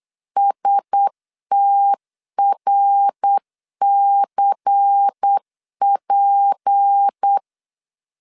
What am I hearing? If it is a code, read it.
Morse code, heard as a keyed tone, about 8 wpm, 800 Hz. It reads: STRCP
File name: LEZIONI / 6 PKX0 / 6 PK 8wpm 3003